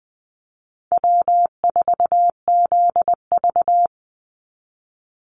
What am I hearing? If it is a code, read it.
Morse code, heard as a keyed tone, about 20 wpm, 700 Hz. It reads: W4ZV
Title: NA SOTA Chaser Callsigns 20wpm